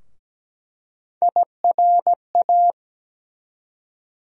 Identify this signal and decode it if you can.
Morse code, heard as a keyed tone, about 17 wpm, 700 Hz. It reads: IRA
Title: USA Call Sign Postfixes - 3 Letters 17wpm